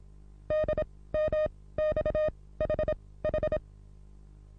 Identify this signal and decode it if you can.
Morse code, heard as a keyed tone, about 34 wpm, spaced as for 19 wpm, 620 Hz. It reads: DMXHH
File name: repos / Morsekurs / DJ4UF Morsen27